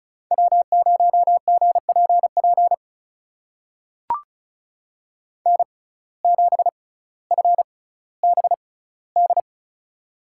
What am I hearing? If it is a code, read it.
Morse code, heard as a keyed tone, about 34 wpm, spaced as for 18 wpm, 700 Hz. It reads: W0GPP N7FBD
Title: USA Call Signs - 5 Characters - 6x Character Spacing - 35wpm